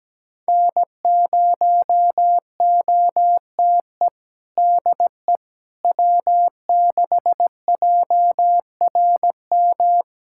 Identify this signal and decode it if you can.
Morse code, heard as a keyed tone, about 17 wpm, 700 Hz. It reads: N0OTE DE W6JRM
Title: USA Callsign DE Callsign - 5 Characters - 17wpm